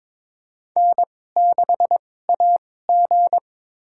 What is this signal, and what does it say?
Morse code, heard as a keyed tone, about 22 wpm, spaced as for 16 wpm, 700 Hz. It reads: N6AG
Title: USA Call Signs - 4 Characters - 2x Character Spacing - 22wpm